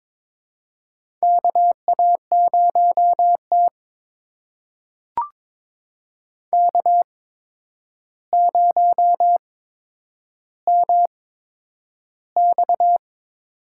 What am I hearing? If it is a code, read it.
Morse code, heard as a keyed tone, about 22 wpm, spaced as for 9 wpm, 700 Hz. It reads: KA0T K0MX
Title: USA Call Signs - 4 Characters - 8x Character Spacing - 22wpm